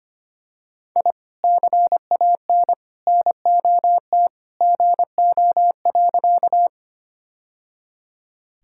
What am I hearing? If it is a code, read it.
Morse code, heard as a keyed tone, about 25 wpm, 700 Hz. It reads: I CAN NOT GO.
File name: practice / 25wpm / Sentences with 2-Word Phrases - 25wpm